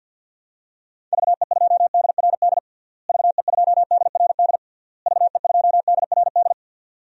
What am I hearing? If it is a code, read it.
Morse code, heard as a keyed tone, about 52 wpm, 700 Hz. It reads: VE2DRD VE2DRD VE2DRD
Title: DXCC-100 Call Signs - 6 Characters - Repeated 3x - 50wpm